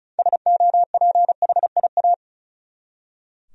Morse code, heard as a keyed tone, about 34 wpm, 700 Hz. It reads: SOPHIA
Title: USA Common Names - 35wpm